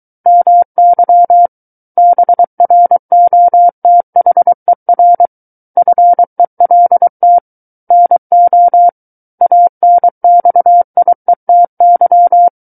Morse code, heard as a keyed tone, about 23 wpm, 700 Hz. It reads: MY BROTHER FELT NO ANXIETY
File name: Book One / 14-Chapter 14 - 23 WPM